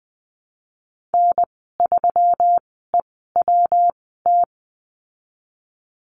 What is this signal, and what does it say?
Morse code, heard as a keyed tone, about 20 wpm, spaced as for 15 wpm, 700 Hz. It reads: N3EWT
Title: 179 Lesson 29g - Slash - Call Signs 20wpm